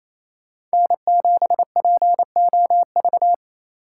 Morse code, heard as a keyed tone, about 35 wpm, 700 Hz. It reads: N7POV
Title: Speed Racing - USA Call Signs 28wpm